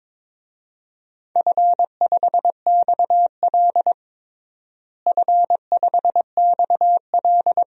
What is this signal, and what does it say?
Morse code, heard as a keyed tone, about 22 wpm, 700 Hz. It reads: F5XL F5XL
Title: DXCC-100 Call Signs - 4 Characters - Repeated 3x - 22wpm